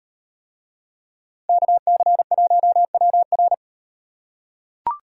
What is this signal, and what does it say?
Morse code, heard as a keyed tone, about 36 wpm, 700 Hz. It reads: KC1WR